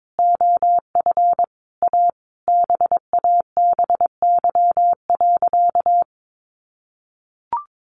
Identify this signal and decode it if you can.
Morse code, heard as a keyed tone, about 22 wpm, 700 Hz. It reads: OF A BABY.
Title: Sentences from Top 1400 Words 22wpm